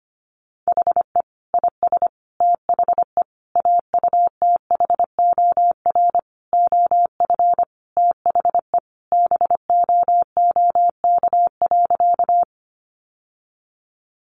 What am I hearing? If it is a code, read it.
Morse code, heard as a keyed tone, about 25 wpm, 700 Hz. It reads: HE IS THE AUTHOR OF THE BOOK.
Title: Sentences from Top 1200 Words 25wpm